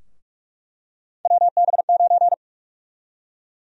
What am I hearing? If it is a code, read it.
Morse code, heard as a keyed tone, about 45 wpm, 700 Hz. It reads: WB9
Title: USA Call Sign Prefixes - 45wpm